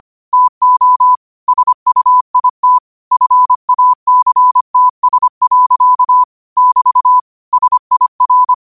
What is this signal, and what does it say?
Morse code, heard as a keyed tone, about 25 wpm, 1000 Hz. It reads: TO SUIT FACTS. <BT> SIR